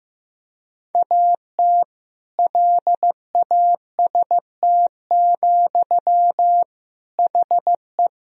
Morse code, heard as a keyed tone, about 15 wpm, 700 Hz. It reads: AT LAST, HE
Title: Sentences from Top 1000 Words - Review Part 3 - 15wpm